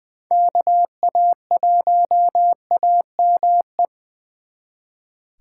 Morse code, heard as a keyed tone, about 24 wpm, 700 Hz. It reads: KA1AME
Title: Speed Racing - USA 2x3 Call Signs 20wpm